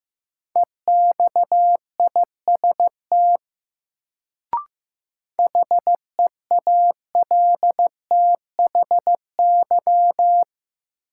Morse code, heard as a keyed tone, about 15 wpm, 700 Hz. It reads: EXIST HEALTHY
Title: Sentences from Top 1000 Words - Review Part 5 - New Word Drill - 15wpm